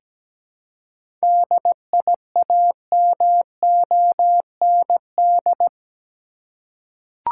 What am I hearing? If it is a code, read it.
Morse code, heard as a keyed tone, about 17 wpm, 700 Hz. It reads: DIAMOND
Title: Seven Letter Words - 17wpm